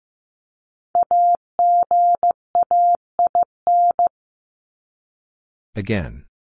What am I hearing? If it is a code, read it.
Morse code, heard as a keyed tone, about 15 wpm, 700 Hz. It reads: AGAIN
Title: Top 200 Words 15wpm